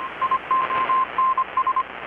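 Morse code, heard as a keyed tone, about 25 wpm, 1050 Hz, with radio noise. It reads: IONS
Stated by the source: Morse from KPH's Night of Nights XX maritime radiotelegraphy event, received in Chiapas, Mexico about 3,800 km away on a Kenwood R-600 with a grounded 10 m long-wire antenna.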